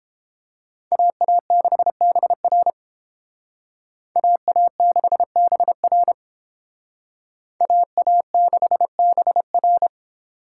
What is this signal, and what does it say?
Morse code, heard as a keyed tone, about 31 wpm, 700 Hz. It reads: AA6BR AA6BR AA6BR